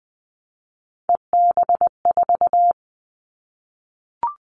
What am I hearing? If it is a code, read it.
Morse code, heard as a keyed tone, about 20 wpm, 700 Hz. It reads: EB4
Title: Callsign Prefixes for Top 25 DXCC - 20wpm